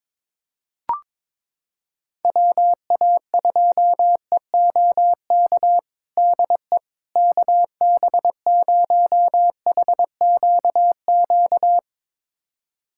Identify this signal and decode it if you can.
Morse code, heard as a keyed tone, about 22 wpm, 700 Hz. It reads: WA2EOK DE KB0HQQ
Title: USA Callsign DE Callsign - 6 Characters - 22wpm